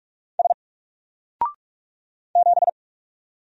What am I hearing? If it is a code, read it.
Morse code, heard as a keyed tone, about 44 wpm, 700 Hz. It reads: S 7
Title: Single Character (Letter:Number:Punctuation:Prosign) 45wpm